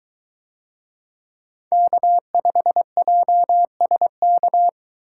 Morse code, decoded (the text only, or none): K5JSK